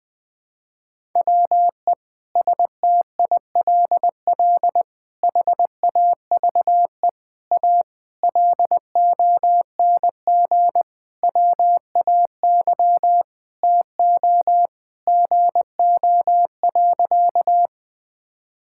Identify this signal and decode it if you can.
Morse code, heard as a keyed tone, about 20 wpm, 700 Hz. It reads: WE STILL HAVE A LONG WAY TO GO.
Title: Sentences with 5-Word Phrases - 20wpm